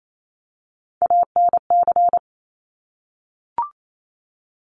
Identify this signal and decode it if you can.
Morse code, heard as a keyed tone, about 28 wpm, 700 Hz. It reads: ANC
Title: Top Three Letter Combinations in Words 28wpm